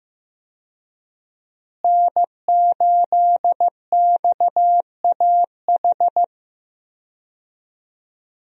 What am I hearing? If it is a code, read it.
Morse code, heard as a keyed tone, about 15 wpm, 700 Hz. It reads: N8XAH
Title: USA Call Signs - 5 Characters - Repeated 3x - 15wpm